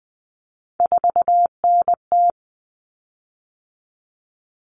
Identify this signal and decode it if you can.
Morse code, heard as a keyed tone, about 20 wpm, 700 Hz. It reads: 4NT
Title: Common Cut Numbers - 3-Digit 20wpm